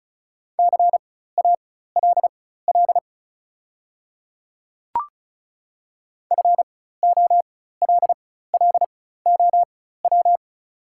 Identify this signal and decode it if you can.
Morse code, heard as a keyed tone, about 35 wpm, spaced as for 16 wpm, 700 Hz. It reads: CALL FOLLOW
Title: 4x Character Spacing - Top 300 Words - 35wpm